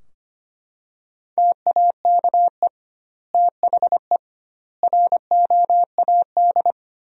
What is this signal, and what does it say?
Morse code, heard as a keyed tone, about 25 wpm, 700 Hz. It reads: TAKE THE ROAD
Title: 2x Word-Spacing - Sentences from Top 400 Words - 25wpm